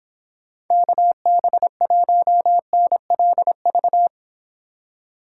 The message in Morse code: KB1NLV

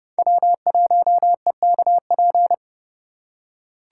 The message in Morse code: W1EKP

